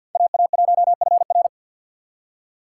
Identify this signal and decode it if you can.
Morse code, heard as a keyed tone, about 52 wpm, 700 Hz. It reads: AA1FR
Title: USA Call Signs - 5 Characters - 3x Character Spacing - 50wpm